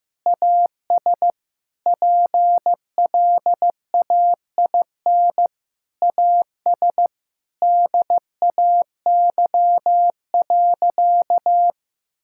Morse code, heard as a keyed tone, about 15 wpm, 700 Hz. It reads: AS PLAIN AS DAY.
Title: Sentences from Top 1000 Words - Review Part 5 - 15wpm